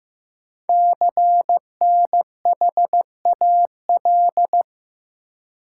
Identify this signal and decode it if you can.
Morse code, heard as a keyed tone, about 15 wpm, 700 Hz. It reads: CNHAL
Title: Code Groups - 5 Letters - 15wpm